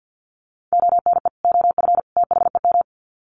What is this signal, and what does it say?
Morse code, heard as a keyed tone, about 50 wpm, 700 Hz. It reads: ONE OF THEM
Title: Sets of 3 Words 50wpm